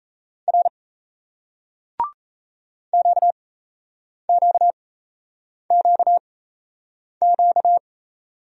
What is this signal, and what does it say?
Morse code, heard as a keyed tone, about 38 wpm, spaced as for 8 wpm, 700 Hz. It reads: R QQQQ